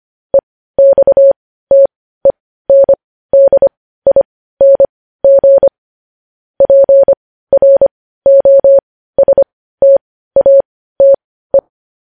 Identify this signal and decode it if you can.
Morse code, heard as a keyed tone, about 25 wpm, spaced as for 15 wpm, 550 Hz. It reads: EXTENDING PROSTATE
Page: news-headlines-in-morse-code-at-15-wpm